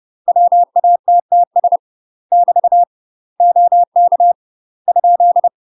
Morse code, heard as a keyed tone, about 30 wpm, 700 Hz. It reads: WATTS <BT> OK ?
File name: UK MASTER QSOs TRAINING / 30WPM / UK MASTER QSOs 0014